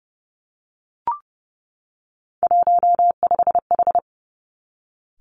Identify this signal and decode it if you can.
Morse code, heard as a keyed tone, about 30 wpm, 700 Hz. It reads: 15H
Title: Three Challenging Letter-Number Combinations 30wpm